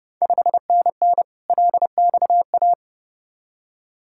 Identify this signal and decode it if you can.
Morse code, heard as a keyed tone, about 32 wpm, 700 Hz. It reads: 5NN LXA
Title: Ontario QSO Party Exchange 30wpm